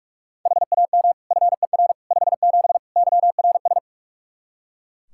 Morse code, heard as a keyed tone, about 42 wpm, 700 Hz. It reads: HAM FER 57 YRS
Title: K1USN SST QSO Practice 45wpm